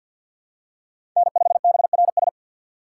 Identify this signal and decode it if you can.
Morse code, heard as a keyed tone, about 53 wpm, 700 Hz. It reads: N5BRS
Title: USA Call Signs - 6x Character Spacing - 50wpm